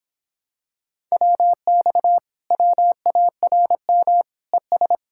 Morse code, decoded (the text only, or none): WX WARM ES